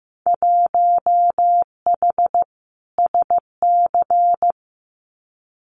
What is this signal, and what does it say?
Morse code, heard as a keyed tone, about 15 wpm, 700 Hz. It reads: 1H SC
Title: Winter Field Day Exchanges 15wpm